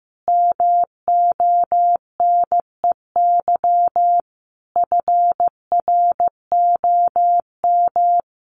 Morse code, heard as a keyed tone, about 15 wpm, 700 Hz. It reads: MONEY FROM
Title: Sentences from Top 1500 Words 15wpm